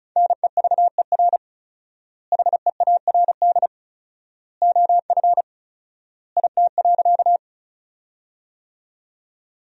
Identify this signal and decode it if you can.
Morse code, heard as a keyed tone, about 36 wpm, 700 Hz. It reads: NEVER HEARD OF IT.
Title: 4x Word-Spacing - Sentences from Top 400 Words - 35wpm